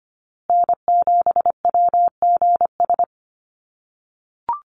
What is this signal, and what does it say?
Morse code, heard as a keyed tone, about 25 wpm, 700 Hz. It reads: N7WGS
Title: USA Call Signs Repeated 3x 25wpm